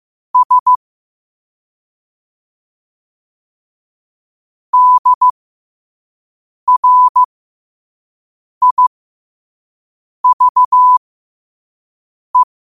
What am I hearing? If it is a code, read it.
Morse code, heard as a keyed tone, about 15 wpm, spaced as for 5 wpm, 1000 Hz. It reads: S DRIVE
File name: qotdfiles / 5wpm